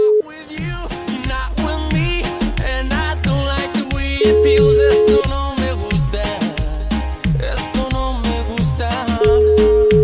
Ringback tone (Europe, 425 Hz), with noise.